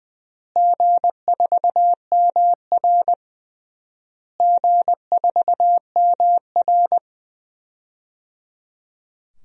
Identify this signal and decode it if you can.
Morse code, heard as a keyed tone, about 20 wpm, 700 Hz. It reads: G4MR G4MR
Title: DXCC-100 Call Signs - 4 Characters - Repeated 3x - 20wpm